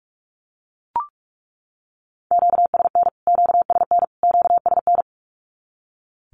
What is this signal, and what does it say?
Morse code, heard as a keyed tone, about 44 wpm, 700 Hz. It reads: QSN QSN QSN
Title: Q-Codes Repeated 3x 45wpm